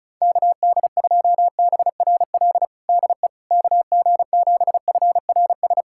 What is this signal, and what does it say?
Morse code, heard as a keyed tone, about 36 wpm, 700 Hz. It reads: KD2BRL DE KG7FRS
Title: USA Callsign DE Callsign 35wpm